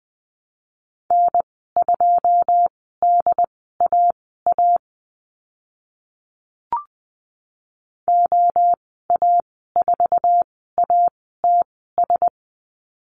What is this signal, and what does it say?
Morse code, heard as a keyed tone, about 20 wpm, spaced as for 15 wpm, 700 Hz. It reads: N2DAA OA4ATS
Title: 082 Lesson 14g - Number 2 - Call Signs 20wpm